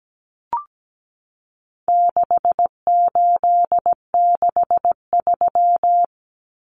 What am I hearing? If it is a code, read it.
Morse code, heard as a keyed tone, about 17 wpm, 700 Hz. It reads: 6863
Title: Common Cut Numbers - 4-Digit 17wpm